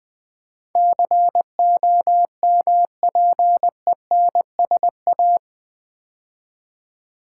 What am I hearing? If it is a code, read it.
Morse code, heard as a keyed tone, about 20 wpm, 700 Hz. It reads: COMPENSA
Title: Build A Word - 4 Letters At A Time - 12-Letter Words - 20wpm